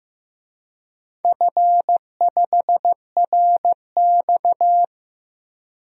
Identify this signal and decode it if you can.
Morse code, heard as a keyed tone, about 15 wpm, 700 Hz. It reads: F5RX
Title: DXCC-100 Call Signs - 4 Characters - 15wpm